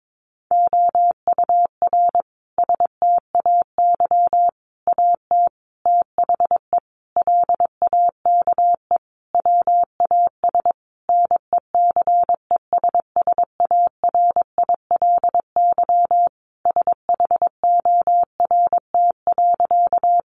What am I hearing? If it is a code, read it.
Morse code, heard as a keyed tone, about 22 wpm, 700 Hz. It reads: OUR STAY AT THE LAKE WAS NECESSARILY SHORT.
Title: Sentences from Top 1400 Words 22wpm